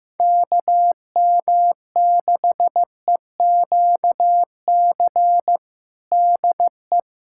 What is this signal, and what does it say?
Morse code, heard as a keyed tone, about 15 wpm, 700 Hz. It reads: KM6EQC DE
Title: USA Callsign DE Callsign 15wpm